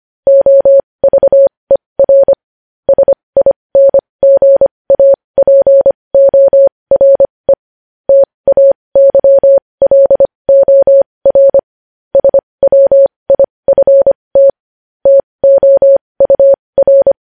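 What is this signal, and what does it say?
Morse code, heard as a keyed tone, about 25 wpm, spaced as for 20 wpm, 550 Hz. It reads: OVER SINGAPORE TAYLOR SWIFT TOUR
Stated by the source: Morse news headlines sent at 20, opening with vvv.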